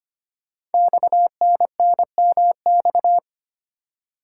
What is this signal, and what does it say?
Morse code, heard as a keyed tone, about 25 wpm, 700 Hz. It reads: XNNMX
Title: Code Groups - 5 Letters - 25wpm